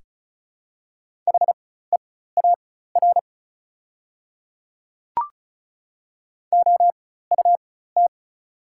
Morse code, heard as a keyed tone, about 34 wpm, spaced as for 16 wpm, 690 Hz. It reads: HEAR OUT
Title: 4x Character Spacing - Top 300 Words - 35wpm